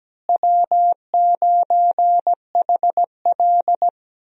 Morse code, heard as a keyed tone, about 17 wpm, 700 Hz. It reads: W9HL
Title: USA Call Signs - 4 Characters - Repeated 2x - 17wpm